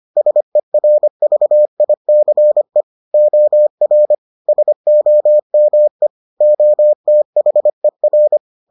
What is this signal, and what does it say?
Morse code, heard as a keyed tone, about 25 wpm, 600 Hz. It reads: SERVICE OR SOME OTHER